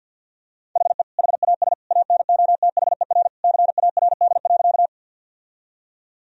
Morse code, heard as a keyed tone, about 50 wpm, 700 Hz. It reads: HE HAS ANOTHER CARD.
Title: Sentences from Top 1000 Words - Review Part 4 - 50wpm